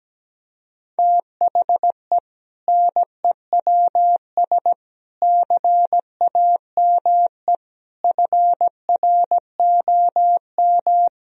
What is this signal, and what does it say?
Morse code, heard as a keyed tone, about 17 wpm, 700 Hz. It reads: THE NEWS CAME FROM
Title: Sentences from Top 1000 Words - Review Part 1 - 17wpm